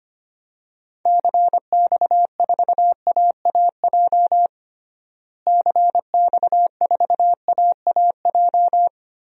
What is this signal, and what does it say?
Morse code, heard as a keyed tone, about 25 wpm, 700 Hz. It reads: CX4AAJ CX4AAJ
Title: DXCC-100 Call Signs - 6 Characters - Repeated 3x - 25wpm